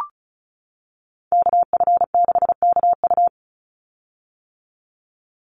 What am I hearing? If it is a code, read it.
Morse code, heard as a keyed tone, about 34 wpm, 700 Hz. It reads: KF6KU
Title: USA Call Signs 35wpm